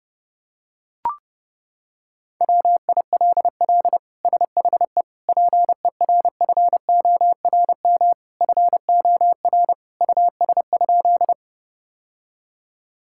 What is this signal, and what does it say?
Morse code, heard as a keyed tone, about 30 wpm, 700 Hz. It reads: WILL SHE PERFORM FOR US?
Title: Sentences from Top 1000 Words - Review Part 5 - 30wpm